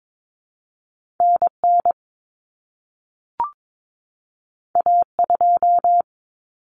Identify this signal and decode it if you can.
Morse code, heard as a keyed tone, about 22 wpm, 700 Hz. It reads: NN A2
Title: Common Cut Numbers - 2-Digit 22wpm